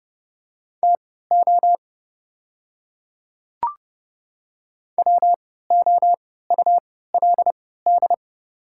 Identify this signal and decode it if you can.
Morse code, heard as a keyed tone, about 30 wpm, spaced as for 17 wpm, 700 Hz. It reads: TO WOULD
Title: 3x Character Spacing - Top 100 Words - 30wpm